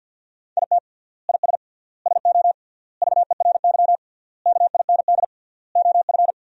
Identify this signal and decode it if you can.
Morse code, heard as a keyed tone, about 50 wpm, 700 Hz. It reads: IT IS SO VERY KIND OF
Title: 3x Word-Spacing - Sentences from Top 300 Words - 50wpm